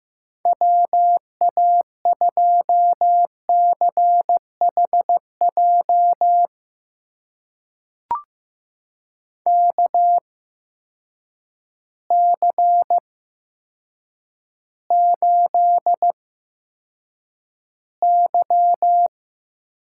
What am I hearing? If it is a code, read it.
Morse code, heard as a keyed tone, about 15 wpm, spaced as for 8 wpm, 700 Hz. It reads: WA2CHJ KC8Y
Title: USA Call Signs - 6 Characters - 8x Character Spacing - 15wpm